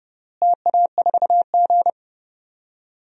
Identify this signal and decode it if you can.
Morse code, heard as a keyed tone, about 30 wpm, 700 Hz. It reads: TA4G
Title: DXCC-100 Call Signs - 4 Characters - Repeated 3x - 30wpm